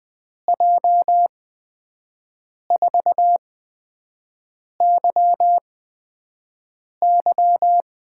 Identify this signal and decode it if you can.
Morse code, heard as a keyed tone, about 20 wpm, spaced as for 5 wpm, 700 Hz. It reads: J4YY
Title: USA Call Signs - 5 Characters - 8x Character Spacing - 20wpm